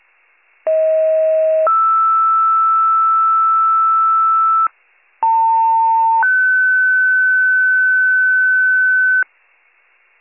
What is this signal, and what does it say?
Two-tone page, heard as a one-second tone, then a three-second tone, about 640 then 1320 Hz, about 900 then 1500 Hz. It is heard from 0.7 to 9.2 s.